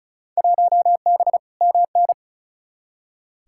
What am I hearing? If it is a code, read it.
Morse code, heard as a keyed tone, about 36 wpm, 700 Hz. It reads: 1B MN
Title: Field Day Exchanges - 35wpm